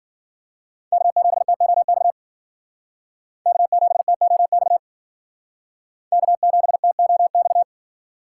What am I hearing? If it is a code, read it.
Morse code, heard as a keyed tone, about 51 wpm, 700 Hz. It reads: K7TOX K7TOX K7TOX